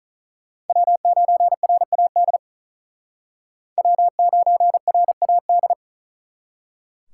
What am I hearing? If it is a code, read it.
Morse code, heard as a keyed tone, about 45 wpm, 700 Hz. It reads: W9RAD W9RAD